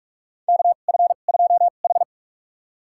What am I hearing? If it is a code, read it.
Morse code, heard as a keyed tone, about 42 wpm, spaced as for 31 wpm, 700 Hz. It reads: KF2H
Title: USA Call Signs - 4 Characters - 2x Character Spacing - 45wpm